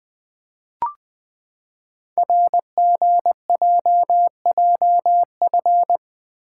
Morse code, heard as a keyed tone, about 20 wpm, 700 Hz. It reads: RGJJF
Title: Code Groups - 5 Letters - 20wpm